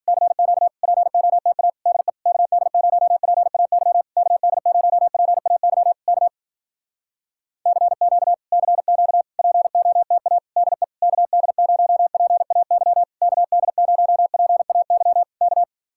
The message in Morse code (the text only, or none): CQ POTA DE KD0PAY KD0PAY K CQ CQ POTA DE KD0PAY KD0PAY K